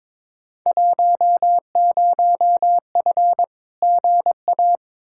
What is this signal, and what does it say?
Morse code, heard as a keyed tone, about 22 wpm, 700 Hz. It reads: 10F GA
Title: Field Day Exchanges Encore - 22wpm